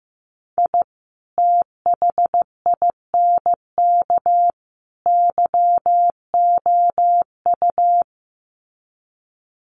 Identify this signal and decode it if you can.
Morse code, heard as a keyed tone, about 15 wpm, 700 Hz. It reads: I THINK YOU
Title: Sets of 3 Words 15wpm